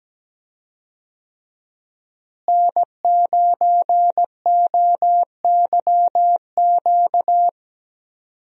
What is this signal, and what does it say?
Morse code, heard as a keyed tone, about 17 wpm, 700 Hz. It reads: N9OYQ
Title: USA Call Signs - 5 Characters - Repeated 3x - 17wpm